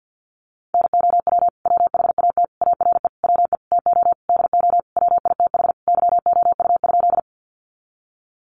Morse code, heard as a keyed tone, about 50 wpm, 700 Hz. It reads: NOW WHAT ARE WE TO DO WITH YOU?
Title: Sentences from Top 100 Words 50wpm